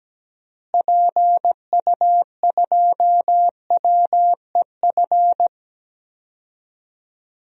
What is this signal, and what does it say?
Morse code, heard as a keyed tone, about 17 wpm, 700 Hz. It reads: PU2WEF
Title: DXCC-100 Call Signs - 6 Characters - Repeated 3x - 17wpm